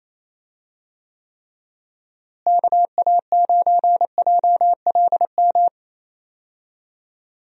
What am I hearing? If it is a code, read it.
Morse code, heard as a keyed tone, about 27 wpm, 700 Hz. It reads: KA9JLM
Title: USA Call Signs - 6 Characters - Repeated 2x - 28wpm